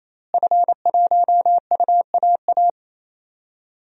Morse code, heard as a keyed tone, about 28 wpm, 700 Hz. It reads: F1UAA
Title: DXCC-100 Call Signs - 5 Characters - Repeated 3x - 28wpm